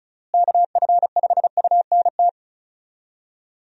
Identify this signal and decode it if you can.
Morse code, heard as a keyed tone, about 36 wpm, 700 Hz. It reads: KF5UNT
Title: USA Call Signs - 6 Characters - Repeated 3x - 35wpm